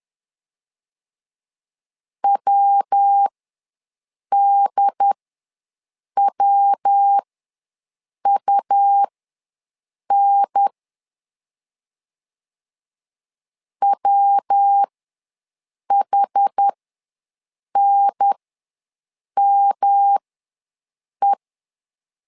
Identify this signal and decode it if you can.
Morse code, heard as a keyed tone, about 10 wpm, spaced as for 6 wpm, 800 Hz. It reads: WDWUN WHNME
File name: LEZIONI / 3 MWRO / 3 MW ws 10wpm 2213